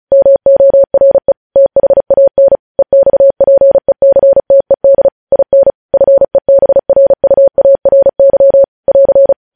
Morse code, heard as a keyed tone, about 36 wpm, 550 Hz. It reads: MORE THAN EXPECTED IN FEBRUARY <AR>
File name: podcast / News headlines-1x-35x35